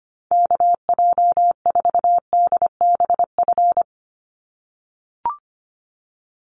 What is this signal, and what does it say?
Morse code, heard as a keyed tone, about 25 wpm, 700 Hz. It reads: KJ4DBF
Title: USA Call Signs 25wpm